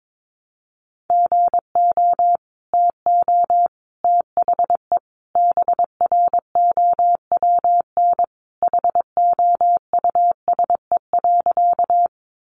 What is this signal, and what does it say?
Morse code, heard as a keyed tone, about 22 wpm, 700 Hz. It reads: GO TO THE BROWN HOUSE.
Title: Sentences from Top 1200 Words 22wpm